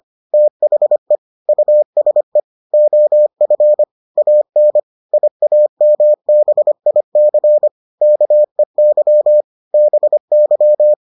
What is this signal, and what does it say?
Morse code, heard as a keyed tone, about 25 wpm, 600 Hz. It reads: THE USE OF AN IAMBIC KEY BY